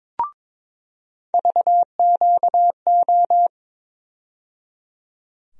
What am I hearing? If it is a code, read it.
Morse code, heard as a keyed tone, about 22 wpm, 700 Hz. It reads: VQO
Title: USA Call Sign Postfixes - 3 Letters 22wpm